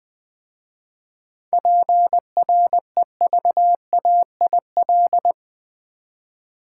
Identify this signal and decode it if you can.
Morse code, heard as a keyed tone, about 20 wpm, 700 Hz. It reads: PREVAIL
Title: 7 Letter Words - First 3 Letters Given - 20wpm